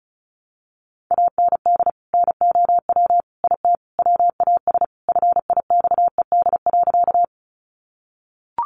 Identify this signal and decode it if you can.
Morse code, heard as a keyed tone, about 34 wpm, 700 Hz. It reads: AND NOW IT WAS FIXED.
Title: Sentences from Top 1100 Words 35wpm